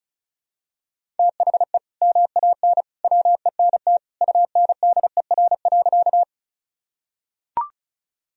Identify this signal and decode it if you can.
Morse code, heard as a keyed tone, about 34 wpm, 700 Hz. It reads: THE MAN WENT UNDER.
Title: Sentences from Top 1000 Words - Review Part 1 - 35wpm